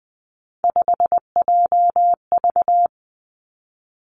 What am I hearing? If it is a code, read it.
Morse code, heard as a keyed tone, about 20 wpm, 700 Hz. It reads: 5JV
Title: Three Challenging Letter-Number Combinations 20wpm